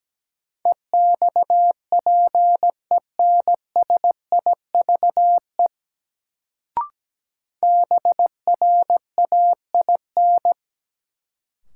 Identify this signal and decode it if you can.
Morse code, heard as a keyed tone, about 17 wpm, 700 Hz. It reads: EXPENSIVE BRAIN
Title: Sentences from Top 1000 Words - Review Part 6 - New Word Drill - 17wpm